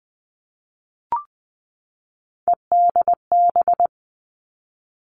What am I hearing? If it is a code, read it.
Morse code, heard as a keyed tone, about 19 wpm, 700 Hz. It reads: EDB
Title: Three Challenging Letter-Number Combinations 20wpm